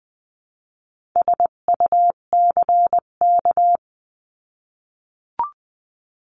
Morse code, heard as a keyed tone, about 20 wpm, 700 Hz. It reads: SUCK
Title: 258 Lesson 42c - Words 16-20wpm